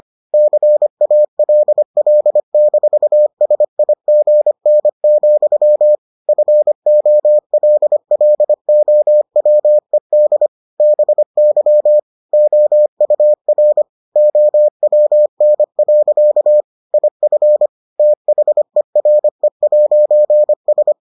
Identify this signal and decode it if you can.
Morse code, heard as a keyed tone, about 25 wpm, 600 Hz. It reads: CALL-SIGN, FOLLOWED BY OUR OWN. IF THERE'S